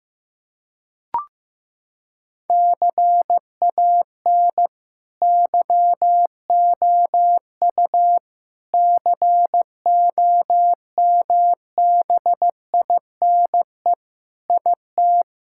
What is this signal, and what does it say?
Morse code, heard as a keyed tone, about 15 wpm, 700 Hz. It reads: CAN YOU COMBINE IT
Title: Sentences from Top 1600 Words 15wpm